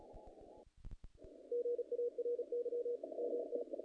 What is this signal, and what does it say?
Morse code, heard as a keyed tone, about 31 wpm, 470 Hz. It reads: GARY MT
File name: contests / naqp-cw-k5tr-op-k5pi- -1800